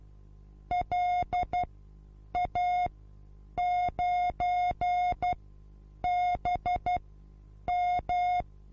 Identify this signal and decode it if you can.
Morse code, heard as a keyed tone, about 12 wpm, spaced as for 8 wpm, 720 Hz. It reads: LA9BM
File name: repos / Morsekurs / DJ4UF Morsen23